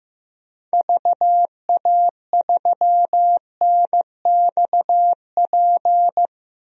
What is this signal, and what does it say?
Morse code, heard as a keyed tone, about 15 wpm, 700 Hz. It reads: VA3NXP
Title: DXCC-100 Call Signs - 6 Characters - Repeated 2x - 15wpm